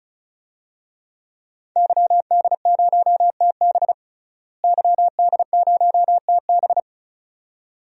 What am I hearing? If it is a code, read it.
Morse code, heard as a keyed tone, about 34 wpm, 700 Hz. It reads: YD0TB YD0TB
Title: DXCC-100 Call Signs - 5 Characters - Repeated 2x - 35wpm